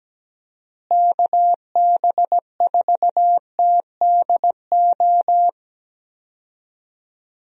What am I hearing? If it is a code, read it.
Morse code, heard as a keyed tone, about 17 wpm, 700 Hz. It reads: KB4TDO
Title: USA Call Signs - 6 Characters - Repeated 3x - 17wpm